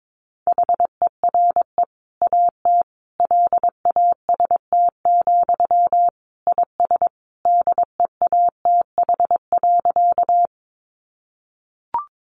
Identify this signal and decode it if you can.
Morse code, heard as a keyed tone, about 22 wpm, 700 Hz. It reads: HERE AT LAST, IS DEATH.
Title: Sentences from Top 1100 Words 22wpm